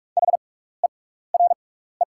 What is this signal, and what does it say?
Morse code, heard as a keyed tone, about 46 wpm, spaced as for 16 wpm, 690 Hz. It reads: HERE
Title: 6x Character Spacing - Top 200 Words - 45wpm